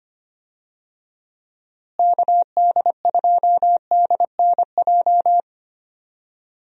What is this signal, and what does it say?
Morse code, heard as a keyed tone, about 25 wpm, 700 Hz. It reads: KD2DNJ
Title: USA Call Signs - 6 Characters - Repeated 2x - 25wpm